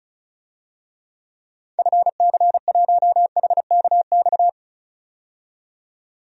Morse code, heard as a keyed tone, about 36 wpm, 700 Hz. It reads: FC1HKX
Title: DXCC-100 Call Signs - 6 Characters - 35wpm